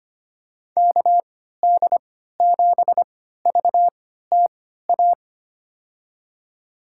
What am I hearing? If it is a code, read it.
Morse code, heard as a keyed tone, about 25 wpm, spaced as for 14 wpm, 700 Hz. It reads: KD7VTA
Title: USA Call Signs - 6 Characters - 3x Character Spacing - 25wpm